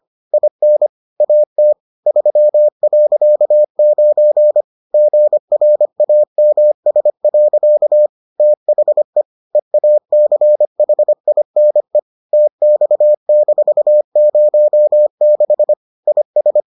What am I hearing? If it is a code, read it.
Morse code, heard as a keyed tone, about 25 wpm, 600 Hz. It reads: IN AT 3.9 GRAMS. THE EACHINE TX-06 IS